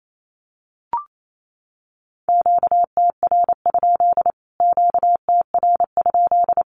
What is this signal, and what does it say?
Morse code, heard as a keyed tone, about 28 wpm, 700 Hz. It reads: QTR? QTR?
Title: Q-Codes Repeated 3x 28wpm